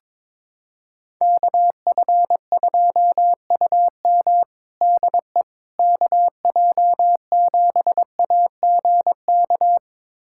Morse code, heard as a keyed tone, about 22 wpm, 700 Hz. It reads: KF2UM DE KJ7AGK
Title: USA Callsign DE Callsign 22wpm